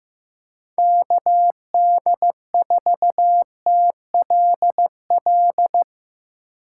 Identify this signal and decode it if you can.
Morse code, heard as a keyed tone, about 16 wpm, 700 Hz. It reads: KD4TLL
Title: USA Call Signs - 6 Characters - 3x Character Spacing - 15wpm